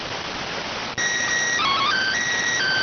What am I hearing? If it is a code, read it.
SSTV (slow-scan television), PD 290.